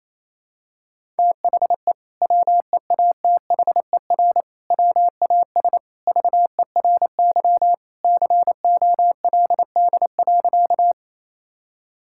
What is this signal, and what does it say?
Morse code, heard as a keyed tone, about 28 wpm, 700 Hz. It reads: THE WEATHER WAS VERY COLD.
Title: Sentences from Top 1000 Words - Review Part 5 - 28wpm